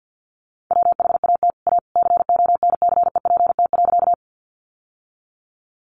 Morse code, heard as a keyed tone, about 50 wpm, 700 Hz. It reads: WHAT A CONCEPT.
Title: Sentences from Top 1100 Words 50wpm